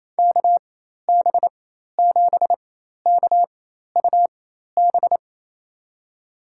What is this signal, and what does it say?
Morse code, heard as a keyed tone, about 28 wpm, spaced as for 13 wpm, 700 Hz. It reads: KB7KUB
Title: USA Call Signs - 4x Character Spacing - 28wpm